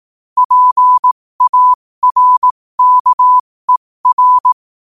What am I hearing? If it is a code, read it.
Morse code, heard as a keyed tone, about 18 wpm, spaced as for 15 wpm, 1000 Hz. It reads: PARKER
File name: qotdfiles / 15wpm